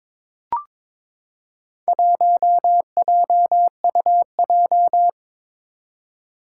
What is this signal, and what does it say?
Morse code, heard as a keyed tone, about 22 wpm, 700 Hz. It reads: 1JUJ
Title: Four Challenging Letter-Number Combinations 22wpm